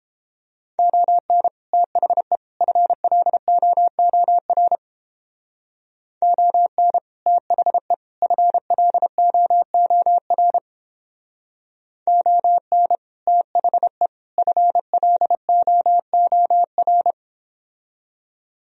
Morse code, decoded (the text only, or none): ON THE FLOOR ON THE FLOOR ON THE FLOOR